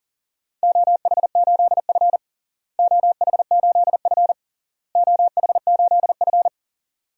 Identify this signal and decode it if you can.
Morse code, heard as a keyed tone, about 40 wpm, 700 Hz. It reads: OH8F OH8F OH8F
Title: DXCC-100 Call Signs - 4 Characters - Repeated 3x - 40wpm